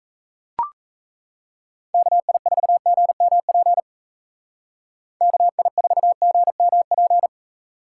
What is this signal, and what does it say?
Morse code, heard as a keyed tone, about 38 wpm, 700 Hz. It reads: KI4GMP KI4GMP